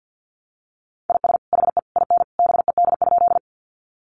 Morse code, heard as a keyed tone, about 50 wpm, 700 Hz. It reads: IS HE IN BED?
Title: Sentences from Top 500 Words 50wpm